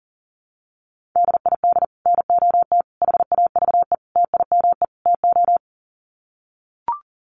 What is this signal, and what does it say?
Morse code, heard as a keyed tone, about 40 wpm, 700 Hz. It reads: DID NOT HAVE TIME TO
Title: Sets of 5 Words Encore 40wpm